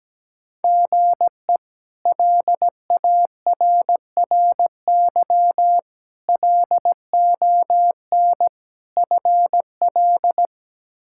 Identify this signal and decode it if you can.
Morse code, heard as a keyed tone, about 17 wpm, 700 Hz. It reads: GE LARRY LON FL
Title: K1USN SST QSO Practice - 17wpm